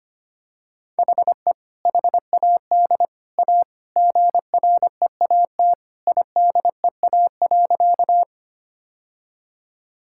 Morse code, heard as a keyed tone, about 25 wpm, 700 Hz. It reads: HE HAD A GREAT IDEA.
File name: practice / 25wpm / Sentences with 3-Word Phrases - 25wpm